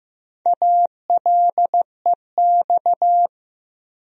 Morse code, heard as a keyed tone, about 15 wpm, 700 Hz. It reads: ALEX
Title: USA Common Names - Max 5 Letters - 15wpm